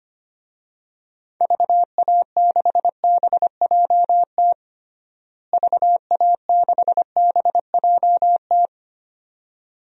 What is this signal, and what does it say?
Morse code, heard as a keyed tone, about 25 wpm, 700 Hz. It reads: VA6BJT VA6BJT
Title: DXCC-100 Call Signs - 6 Characters - Repeated 2x - 25wpm